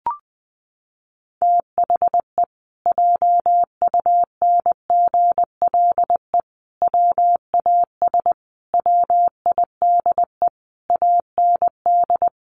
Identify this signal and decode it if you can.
Morse code, heard as a keyed tone, about 20 wpm, 700 Hz. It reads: THE JUNGLE WAS WIDE AND
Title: A Sound Of Thunder 20wpm